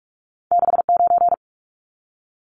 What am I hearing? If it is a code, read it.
Morse code, heard as a keyed tone, about 46 wpm, 700 Hz. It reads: 69
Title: Numbers - 2-Digit 45wpm